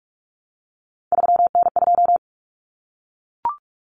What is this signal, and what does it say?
Morse code, heard as a keyed tone, about 44 wpm, 700 Hz. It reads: 3N2
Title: Common Cut Numbers - 3-Digit 45wpm